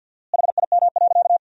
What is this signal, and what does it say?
Morse code, heard as a keyed tone, about 50 wpm, 700 Hz. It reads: HIM.
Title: 4x Word-Spacing - Sentences from Top 200 Words - 50wpm